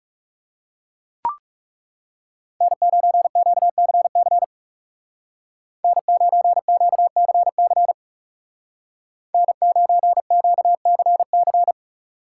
Morse code, heard as a keyed tone, about 36 wpm, 700 Hz. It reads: N9QCC N9QCC N9QCC
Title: Speed Racing - USA 1x3 Call Signs 30wpm